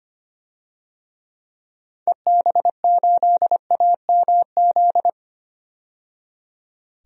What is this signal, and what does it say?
Morse code, heard as a keyed tone, about 25 wpm, 700 Hz. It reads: EB8AMZ
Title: DXCC-100 Call Signs - 6 Characters - 25wpm